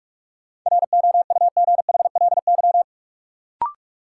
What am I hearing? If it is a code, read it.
Morse code, heard as a keyed tone, about 42 wpm, 700 Hz. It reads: ROUGHLY
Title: Sentences from Top 1500 Words - New Word Drill - 45wpm